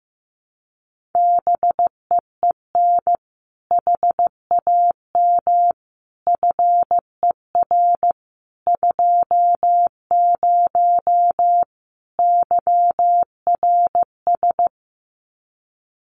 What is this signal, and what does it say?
Morse code, heard as a keyed tone, about 15 wpm, 700 Hz. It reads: BEEN HAM FER 20 YRS
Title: Common QSO Phrases 15wpm